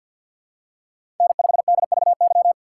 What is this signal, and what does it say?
Morse code, heard as a keyed tone, about 52 wpm, 700 Hz. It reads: N5DVY
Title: USA Call Signs - 5 Characters - 2x Character Spacing - 50wpm